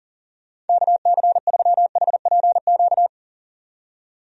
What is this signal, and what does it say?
Morse code, heard as a keyed tone, about 40 wpm, 700 Hz. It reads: KC3HPQ
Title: USA Call Signs - 6 Characters - 40wpm